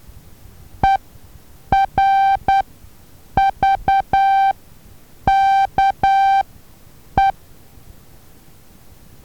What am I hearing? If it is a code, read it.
Morse code, heard as a keyed tone, about 10 wpm, spaced as for 7 wpm, 780 Hz. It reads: ERVKE
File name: Taviro HA6VA B